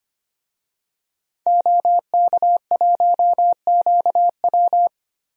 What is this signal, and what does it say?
Morse code, heard as a keyed tone, about 25 wpm, 700 Hz. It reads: OK1QW
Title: DXCC-100 Call Signs - 5 Characters - Repeated 2x - 25wpm